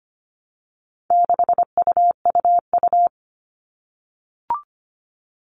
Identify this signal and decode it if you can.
Morse code, heard as a keyed tone, about 25 wpm, 700 Hz. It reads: 6UUU
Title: All Cut Numbers - 4-Digit 25wpm